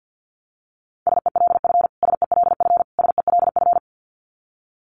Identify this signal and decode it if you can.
Morse code, heard as a keyed tone, about 50 wpm, 700 Hz. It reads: SELF SELF SELF
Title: Top 500 Words Repeated 50wpm